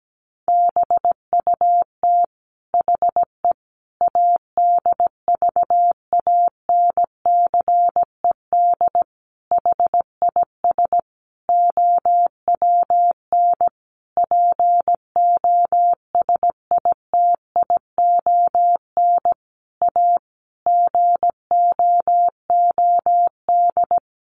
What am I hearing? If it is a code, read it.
Morse code, heard as a keyed tone, about 17 wpm, 700 Hz. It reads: BUT HE ADVANCED HIS OWN POSITION A GOOD
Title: Sentences from Top 1200 Words 17wpm